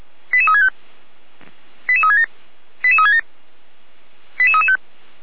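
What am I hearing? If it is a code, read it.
Five-tone selective call (ZVEI), heard as quick runs of five tones, tones of about 70 ms. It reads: 88365 88367 88367 88335
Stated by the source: ZVEI five-tone call received by SDR.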